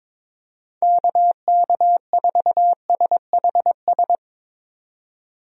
Morse code, heard as a keyed tone, about 22 wpm, 700 Hz. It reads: KK4SHS
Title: USA Call Signs - 6 Characters - 22wpm